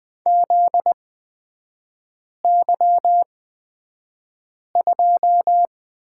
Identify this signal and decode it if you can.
Morse code, heard as a keyed tone, about 20 wpm, spaced as for 5 wpm, 700 Hz. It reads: ZY2
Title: 243 Lesson 39e - Letter z - Three Characters 20wpm